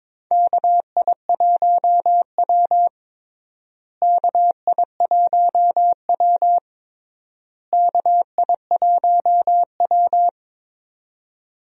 Morse code, heard as a keyed tone, about 21 wpm, 700 Hz. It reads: KI1W KI1W KI1W
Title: USA Call Signs - 4 Characters - Repeated 3x - 22wpm